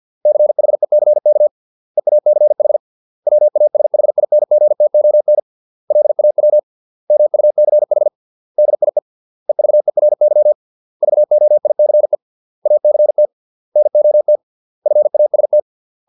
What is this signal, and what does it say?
Morse code, heard as a keyed tone, about 50 wpm, 600 Hz. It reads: CHECK EACH WASHINGTON LAW MUCH DIE EVERY VOICE ACT NOT FAST